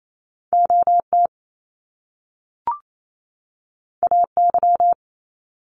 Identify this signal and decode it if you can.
Morse code, heard as a keyed tone, about 28 wpm, 700 Hz. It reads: OT AY
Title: Top Two Letter Combinations in Words 28wpm